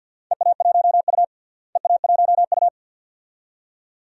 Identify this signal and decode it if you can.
Morse code, heard as a keyed tone, about 53 wpm, 700 Hz. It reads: EA1U EA1U
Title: DXCC-100 Call Signs - 4 Characters - Repeated 2x - 50wpm